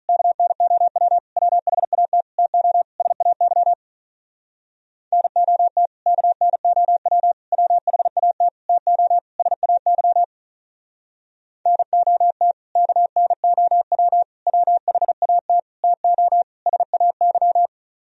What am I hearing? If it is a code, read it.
Morse code, heard as a keyed tone, about 44 wpm, 700 Hz. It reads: KNOW WHAT TO SAY NOT KNOW WHAT TO SAY NOT KNOW WHAT TO SAY